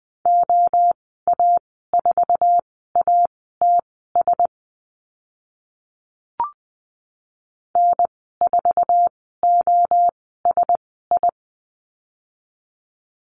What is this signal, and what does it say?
Morse code, heard as a keyed tone, about 20 wpm, spaced as for 14 wpm, 700 Hz. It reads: OA4ATS N4OSI
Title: 062 Lesson 11g - Letter h - Call Signs 20wpm